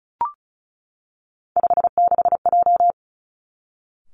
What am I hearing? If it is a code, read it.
Morse code, heard as a keyed tone, about 36 wpm, 700 Hz. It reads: 56J
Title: Three Challenging Letter-Number Combinations 35wpm